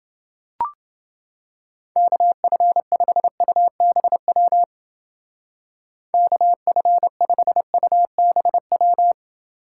Morse code, heard as a keyed tone, about 29 wpm, 700 Hz. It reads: KF5UBW KF5UBW